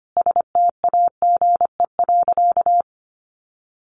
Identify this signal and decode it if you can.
Morse code, heard as a keyed tone, about 25 wpm, 700 Hz. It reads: STAGE.
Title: Sentences from Top 1200 Words 25wpm